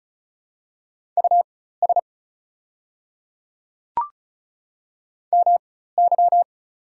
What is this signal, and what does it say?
Morse code, heard as a keyed tone, about 34 wpm, spaced as for 16 wpm, 690 Hz. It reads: US MY
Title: 4x Character Spacing - Top 200 Words - 35wpm